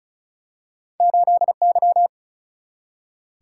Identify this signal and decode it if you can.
Morse code, heard as a keyed tone, about 36 wpm, 700 Hz. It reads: 8 Y
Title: Single-Letter Single-Number 35wpm